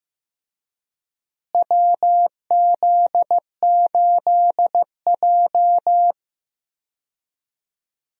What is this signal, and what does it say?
Morse code, heard as a keyed tone, about 15 wpm, 700 Hz. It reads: WZ8J
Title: USA Call Signs - 4 Characters - Repeated 2x - 15wpm